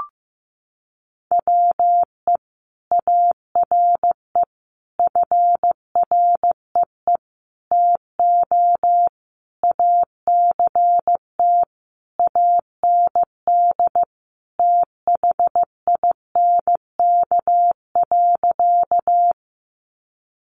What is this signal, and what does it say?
Morse code, heard as a keyed tone, about 15 wpm, 700 Hz. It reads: WE ARE FREE TO ACT AND THINK.
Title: Sentences from Top 500 Words 15wpm